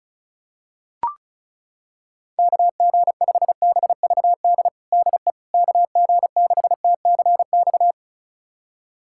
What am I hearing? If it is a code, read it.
Morse code, heard as a keyed tone, about 34 wpm, 700 Hz. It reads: KG5BVD DE KG6TCX
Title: USA Callsign DE Callsign 35wpm